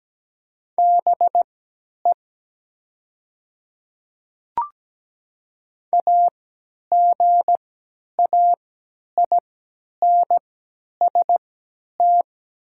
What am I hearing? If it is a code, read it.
Morse code, heard as a keyed tone, about 17 wpm, spaced as for 10 wpm, 700 Hz. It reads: BE AGAINST
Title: 3x Character Spacing - Top 500 Words - 17wpm